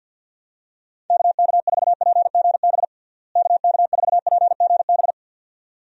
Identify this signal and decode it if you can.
Morse code, heard as a keyed tone, about 46 wpm, 700 Hz. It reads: KK4PGB KK4PGB
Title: USA Call Signs - 6 Characters - Repeated 2x - 50wpm